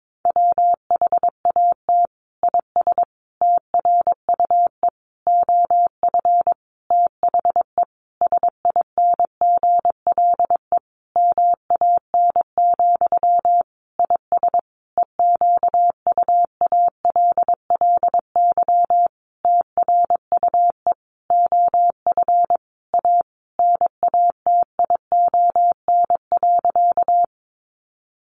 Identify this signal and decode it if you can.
Morse code, heard as a keyed tone, about 22 wpm, 700 Hz. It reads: WHAT IS TRUE OF THE SINGLE MAN, IS EQUALLY TRUE OF A NATION.
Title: Sentences from Top 1400 Words 22wpm